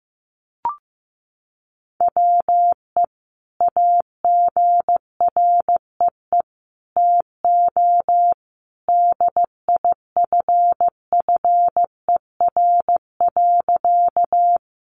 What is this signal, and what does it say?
Morse code, heard as a keyed tone, about 15 wpm, 700 Hz. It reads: WE AGREE TO DIFFER.
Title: Sentences from Top 1100 Words 15wpm